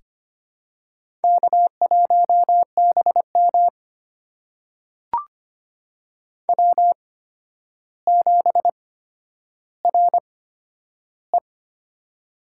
Morse code, heard as a keyed tone, about 25 wpm, spaced as for 11 wpm, 700 Hz. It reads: K1BM W7RE
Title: USA Call Signs - 4 Characters - 8x Character Spacing - 25wpm